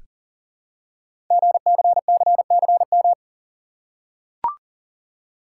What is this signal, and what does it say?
Morse code, heard as a keyed tone, about 38 wpm, 700 Hz. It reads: GCCCM
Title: Code Groups - 5 Letters - 40wpm